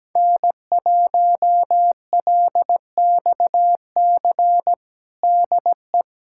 Morse code, heard as a keyed tone, about 17 wpm, 700 Hz. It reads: N1LXC DE
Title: USA Callsign DE Callsign - 5 Characters - 17wpm